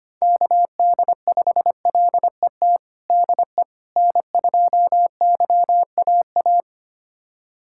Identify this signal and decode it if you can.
Morse code, heard as a keyed tone, about 25 wpm, 700 Hz. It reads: KD5LET DE N2YAA
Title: USA Callsign DE Callsign 25wpm